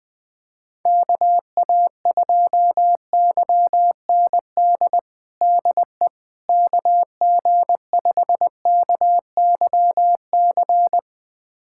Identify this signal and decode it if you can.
Morse code, heard as a keyed tone, about 20 wpm, 700 Hz. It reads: KA2YND DE KG5KYC
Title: USA Callsign DE Callsign - 6 Characters - 20wpm